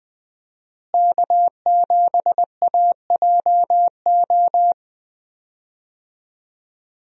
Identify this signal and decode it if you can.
Morse code, heard as a keyed tone, about 21 wpm, 700 Hz. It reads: K7AJO
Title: USA Call Signs - 5 Characters - Repeated 3x - 20wpm